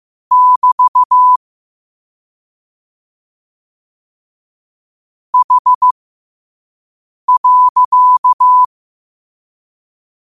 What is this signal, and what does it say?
Morse code, heard as a keyed tone, about 15 wpm, spaced as for 5 wpm, 1000 Hz. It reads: <BT> H.
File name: qotdfiles / 5wpm